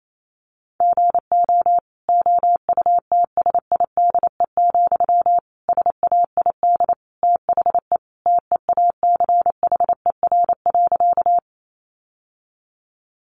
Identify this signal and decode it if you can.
Morse code, heard as a keyed tone, about 28 wpm, 700 Hz. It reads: GO OUTSIDE, SAID THE TEACHER.
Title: Sentences from Top 900 Words 28wpm